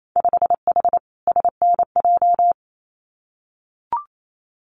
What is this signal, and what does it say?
Morse code, heard as a keyed tone, about 28 wpm, 700 Hz. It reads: 5H SNJ
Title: Winter Field Day Exchanges Encore 28wpm